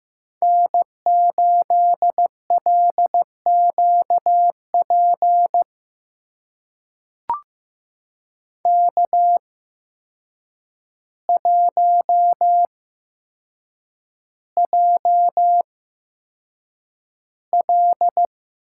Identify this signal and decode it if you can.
Morse code, heard as a keyed tone, about 15 wpm, spaced as for 7 wpm, 700 Hz. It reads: N8LQP K1JL
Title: USA Call Signs - 5 Characters - 8x Character Spacing - 15wpm